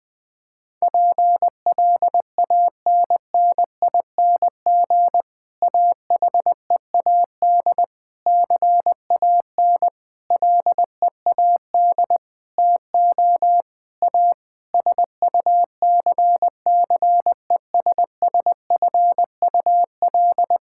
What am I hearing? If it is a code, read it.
Morse code, heard as a keyed tone, about 19 wpm, 700 Hz. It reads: PLANNING AHEAD CAN LEAD TO A SUCCESSFUL
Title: Difficult Sentences with Hints - 20wpm